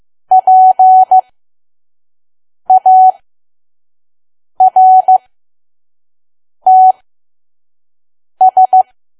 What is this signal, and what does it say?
Morse code, heard as a keyed tone, about 15 wpm, spaced as for 5 wpm, 750 Hz. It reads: PARTS
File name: Archive / 5 WPM files / PM